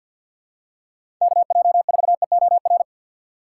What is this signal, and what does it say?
Morse code, heard as a keyed tone, about 50 wpm, 700 Hz. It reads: KJ4EOR
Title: USA Call Signs - 6 Characters - 4x Character Spacing - 50wpm